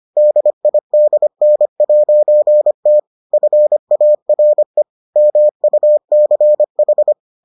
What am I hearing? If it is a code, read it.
Morse code, heard as a keyed tone, about 25 wpm, 600 Hz. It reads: DIDN'T FARE MUCH